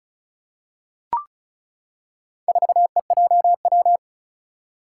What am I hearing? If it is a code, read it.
Morse code, heard as a keyed tone, about 36 wpm, 700 Hz. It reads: 4EJW
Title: Four Challenging Letter-Number Combinations 35wpm